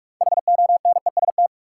Morse code, heard as a keyed tone, about 44 wpm, 700 Hz. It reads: HONEST
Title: Six Letter Words - 45wpm